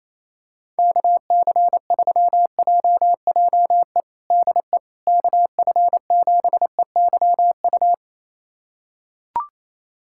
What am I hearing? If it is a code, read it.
Morse code, heard as a keyed tone, about 28 wpm, 700 Hz. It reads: KC3JJE DE KF7EYU
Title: USA Callsign DE Callsign 28wpm